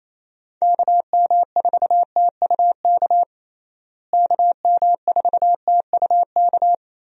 Morse code, heard as a keyed tone, about 28 wpm, 700 Hz. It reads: KM4TUK KM4TUK
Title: USA Call Signs - 6 Characters - Repeated 3x - 28wpm